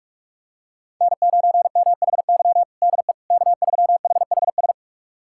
Morse code, heard as a keyed tone, about 46 wpm, 700 Hz. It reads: N9MHY DE K3HHS
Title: USA Callsign DE Callsign - 5 Characters - 45wpm